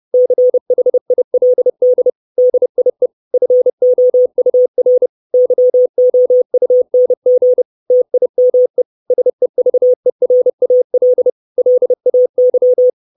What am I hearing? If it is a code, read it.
Morse code, heard as a keyed tone, about 30 wpm, 500 Hz. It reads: CHILD DIE FOUR YOUNG TIME SEVERAL LAY
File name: 632 Words 30WPM